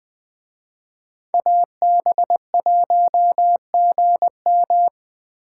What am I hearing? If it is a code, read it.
Morse code, heard as a keyed tone, about 20 wpm, 700 Hz. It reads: AB1GM